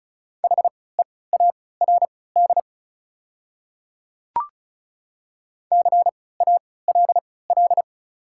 Morse code, heard as a keyed tone, about 34 wpm, spaced as for 20 wpm, 690 Hz. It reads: HEARD CALL
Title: 3x Character Spacing - Top 500 Words - 35wpm